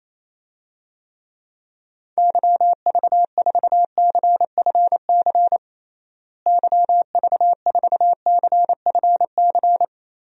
Morse code, heard as a keyed tone, about 28 wpm, 700 Hz. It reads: YV4CFC YV4CFC
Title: DXCC-100 Call Signs - 6 Characters - Repeated 2x - 28wpm